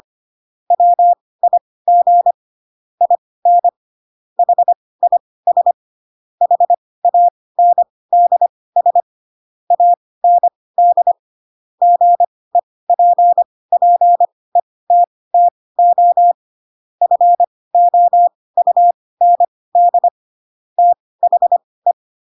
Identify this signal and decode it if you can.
Morse code, heard as a keyed tone, about 25 wpm, spaced as for 18 wpm, 700 Hz. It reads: WIG IN HIS HANDS AND GEPPETTO FOUND THE
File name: Pinocchio / Pinocchio ch2 12 25 ik1hge